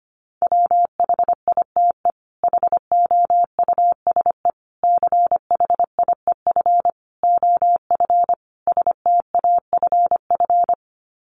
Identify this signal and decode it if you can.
Morse code, heard as a keyed tone, about 25 wpm, 700 Hz. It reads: WHITE HOUSE CHIEF OF STAFF
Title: Sets of 5 Words Encore 25wpm